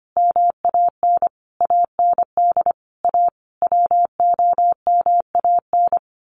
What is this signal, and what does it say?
Morse code, heard as a keyed tone, about 25 wpm, 700 Hz. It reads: MAN AND A WOMAN
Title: Sets of 5 Words 25wpm